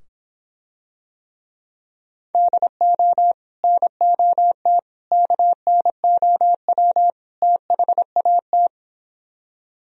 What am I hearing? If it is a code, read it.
Morse code, heard as a keyed tone, about 26 wpm, 700 Hz. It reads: DO NOT KNOW THAT